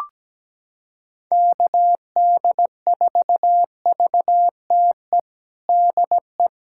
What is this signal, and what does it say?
Morse code, heard as a keyed tone, about 17 wpm, 700 Hz. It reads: KD4VTE DE
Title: USA Callsign DE Callsign - 6 Characters - 17wpm